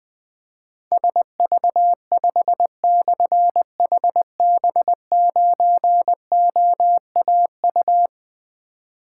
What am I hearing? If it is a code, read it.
Morse code, heard as a keyed tone, about 21 wpm, 700 Hz. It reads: SV5/HB9OAU
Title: Difficult Call Signs 20wpm